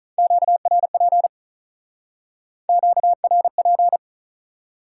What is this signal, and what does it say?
Morse code, heard as a keyed tone, about 47 wpm, 700 Hz. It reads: QRP QRP